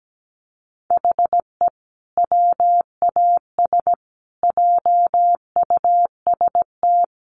Morse code, heard as a keyed tone, about 17 wpm, 700 Hz. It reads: HE WAS JUST
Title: Sentences from Top 1100 Words 17wpm